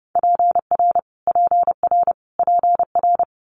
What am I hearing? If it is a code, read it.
Morse code, heard as a keyed tone, about 30 wpm, 700 Hz. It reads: PR PR PR
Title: Multi-letter Phonemes Repeated 3x 30wpm